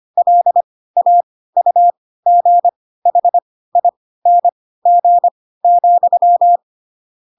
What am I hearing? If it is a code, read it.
Morse code, heard as a keyed tone, about 25 wpm, spaced as for 16 wpm, 700 Hz. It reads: LAUGHING,
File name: Pinocchio / Pinocchio ch3 10 25 ik1hge